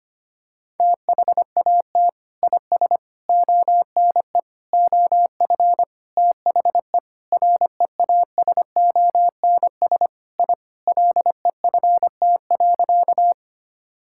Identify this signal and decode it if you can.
Morse code, heard as a keyed tone, about 25 wpm, 700 Hz. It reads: THAT IS ONE OF THE REASONS I LEFT.
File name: practice / 25wpm / Sentences with 4-Word Phrases - 25wpm